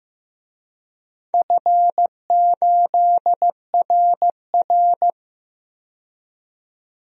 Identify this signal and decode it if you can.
Morse code, heard as a keyed tone, about 15 wpm, 700 Hz. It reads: F8RR
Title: DXCC-100 Call Signs - 4 Characters - 15wpm